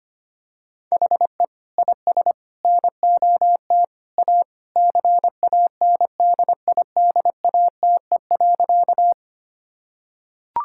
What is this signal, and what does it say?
Morse code, heard as a keyed tone, about 25 wpm, 700 Hz. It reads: HE IS NOT A CANDIDATE.
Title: Sentences from Top 1600 Words 25wpm